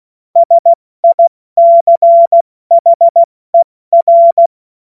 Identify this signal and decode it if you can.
Morse code, heard as a keyed tone, about 16 wpm, spaced as for 14 wpm, 680 Hz. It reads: SICHER
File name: Morsekurs / Witze / Witz 20. Januar 2026 70 F80